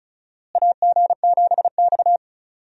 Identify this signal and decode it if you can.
Morse code, heard as a keyed tone, about 36 wpm, 700 Hz. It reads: AG7X
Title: USA Call Signs - 4 Characters - Repeated 3x - 35wpm